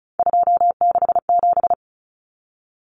Morse code, heard as a keyed tone, about 34 wpm, 700 Hz. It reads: 267
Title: Numbers - 3-Digit 35wpm